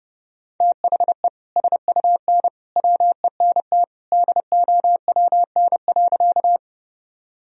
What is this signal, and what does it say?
Morse code, heard as a keyed tone, about 32 wpm, 700 Hz. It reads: THE SUN WENT DOWN.
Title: Sentences from Top 1000 Words - Review Part 1 - 30wpm